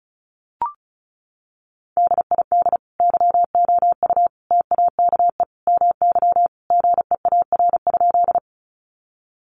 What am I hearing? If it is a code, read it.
Morse code, heard as a keyed tone, about 34 wpm, 700 Hz. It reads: DID YOU TAKE MY GEAR?
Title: Sentences from Top 1400 Words 35wpm